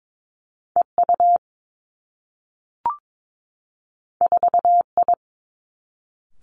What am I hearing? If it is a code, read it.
Morse code, heard as a keyed tone, about 22 wpm, 700 Hz. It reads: E U 4 I
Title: Two Challenging Letter-Number Combinations 22wpm